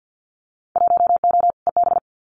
Morse code, heard as a keyed tone, about 50 wpm, 700 Hz. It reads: 1O EB
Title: Winter Field Day Exchanges 50wpm